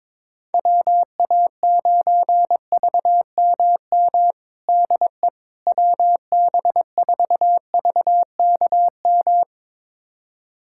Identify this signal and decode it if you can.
Morse code, heard as a keyed tone, about 22 wpm, 700 Hz. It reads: WA9VMM DE WB4VKM
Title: USA Callsign DE Callsign - 6 Characters - 22wpm